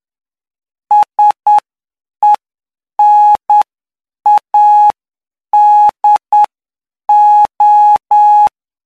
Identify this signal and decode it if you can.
Morse code, heard as a keyed tone, about 9 wpm, spaced as for 7 wpm, 840 Hz. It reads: SENADO
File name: cw / py2lcd / 04-PY2LCD-Li��o 13